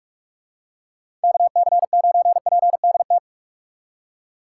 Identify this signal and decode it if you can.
Morse code, heard as a keyed tone, about 44 wpm, 700 Hz. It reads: KC9PDT
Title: USA Call Signs - 6 Characters - 6x Character Spacing - 45wpm